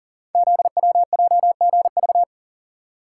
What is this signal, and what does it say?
Morse code, heard as a keyed tone, about 38 wpm, 690 Hz. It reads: ZWJGV
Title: Code Groups - 5 Letters - 40wpm